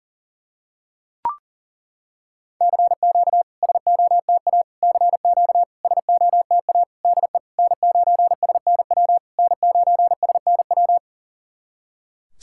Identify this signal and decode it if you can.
Morse code, heard as a keyed tone, about 39 wpm, 700 Hz. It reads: CQ SOTA CQ SOTA DE N9SNW N9SNW
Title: SOTA QSO Practice 40wpm